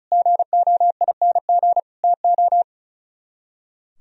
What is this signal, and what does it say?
Morse code, heard as a keyed tone, about 38 wpm, 700 Hz. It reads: GOING TO
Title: Speed-Racing - Sets of 4 Words 35wpm